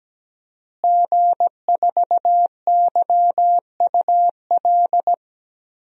Morse code, heard as a keyed tone, about 17 wpm, 700 Hz. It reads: G4YUL
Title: DXCC-100 Call Signs - 5 Characters - Repeated 3x - 17wpm